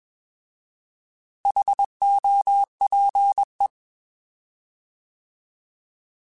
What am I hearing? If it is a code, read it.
Morse code, heard as a keyed tone, about 21 wpm, 800 Hz. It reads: HOPE
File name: Alphabetical / Alphabetical mp3s / Hs @20